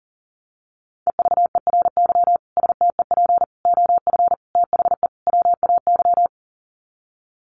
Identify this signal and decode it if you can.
Morse code, heard as a keyed tone, about 40 wpm, 700 Hz. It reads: EVERY STEP OF THE WAY
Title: Sets of 5 Words Encore2 40wpm